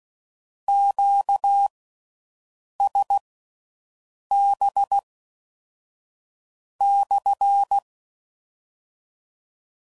Morse code, heard as a keyed tone, about 16 wpm, spaced as for 5 wpm, 800 Hz. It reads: QSB/